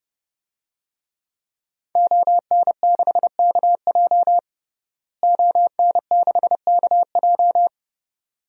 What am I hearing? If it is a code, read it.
Morse code, heard as a keyed tone, about 32 wpm, 700 Hz. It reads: ON6KJ ON6KJ
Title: DXCC-100 Call Signs - 5 Characters - Repeated 2x - 30wpm